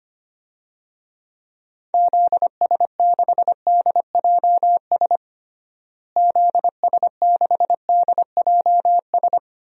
Morse code, heard as a keyed tone, about 25 wpm, 700 Hz. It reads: ZS6DJS ZS6DJS
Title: DXCC-100 Call Signs - 6 Characters - Repeated 2x - 25wpm